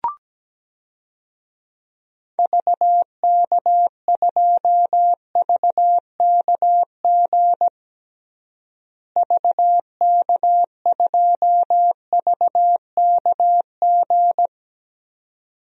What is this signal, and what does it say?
Morse code, heard as a keyed tone, about 17 wpm, 700 Hz. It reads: VK2VKG VK2VKG
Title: DXCC-100 Call Signs - 6 Characters - Repeated 2x - 17wpm